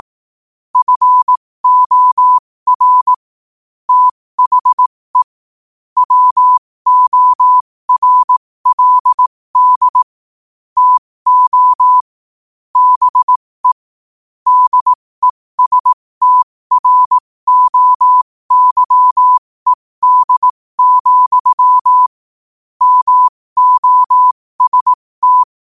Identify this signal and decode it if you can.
Morse code, heard as a keyed tone, about 18 wpm, spaced as for 15 wpm, 1000 Hz. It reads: FOR THE WORLD TO BE DESTROYED, MOST
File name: qotdfiles / 15wpm